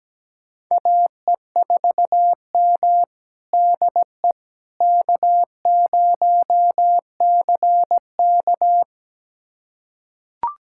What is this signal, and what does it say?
Morse code, heard as a keyed tone, about 17 wpm, 700 Hz. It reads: AE4M DE K0CK
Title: USA Callsign DE Callsign - 4 Characters - 17wpm